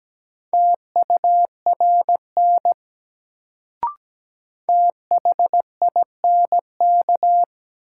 Morse code, heard as a keyed tone, about 17 wpm, 700 Hz. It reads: TURN THINK
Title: Sentences from Top 200 Words - New Word Drill - 17wpm